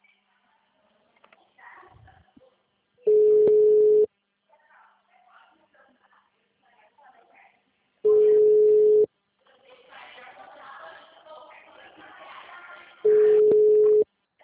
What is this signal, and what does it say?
Ringback tone (Europe, 425 Hz), with speech.